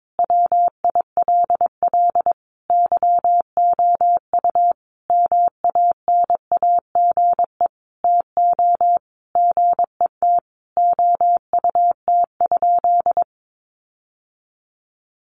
Morse code, heard as a keyed tone, about 22 wpm, 700 Hz. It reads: WILL YOU MANAGE TO GET OUT?
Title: Sentences from Top 1400 Words 22wpm